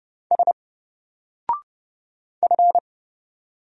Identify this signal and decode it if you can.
Morse code, heard as a keyed tone, about 31 wpm, 700 Hz. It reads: S F
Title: Single Character (Letter:Number:Punctuation:Prosign) 30wpm